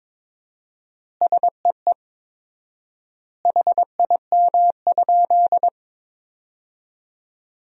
Morse code, heard as a keyed tone, about 22 wpm, 700 Hz. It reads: SEE HIM?
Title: 4x Word-Spacing - Sentences from Top 100 Words - 22wpm